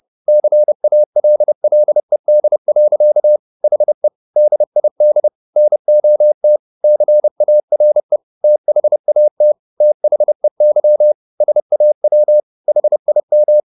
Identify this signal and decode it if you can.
Morse code, heard as a keyed tone, about 30 wpm, 600 Hz. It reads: CALLED. HE DID NOT CARE THAT THEY SAW HIM